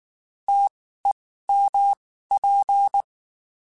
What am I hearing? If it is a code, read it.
Morse code, heard as a keyed tone, about 19 wpm, spaced as for 14 wpm, 800 Hz. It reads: TEMP
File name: (spaced) / QSOs Spaced-mp3s / WD0FBT @18 sp